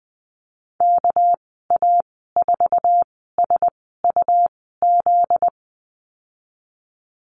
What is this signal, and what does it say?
Morse code, heard as a keyed tone, about 20 wpm, spaced as for 14 wpm, 700 Hz. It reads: KA4SUZ
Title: 245 Lesson 39g - Letter z - Call Signs 20wpm